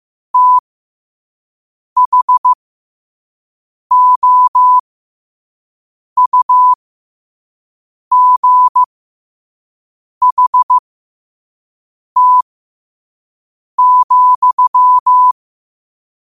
Morse code, heard as a keyed tone, about 15 wpm, spaced as for 5 wpm, 1000 Hz. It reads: THOUGHT,
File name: qotdfiles / 5wpm